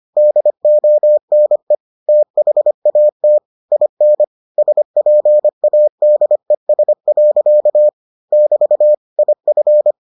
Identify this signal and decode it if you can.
Morse code, heard as a keyed tone, about 25 wpm, 600 Hz. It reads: DONE THAT IN SPADES. <BT> IF